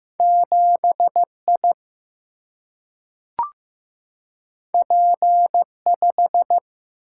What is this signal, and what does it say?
Morse code, heard as a keyed tone, about 15 wpm, 700 Hz. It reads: 7I P5
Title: Single-Letter Single-Number 15wpm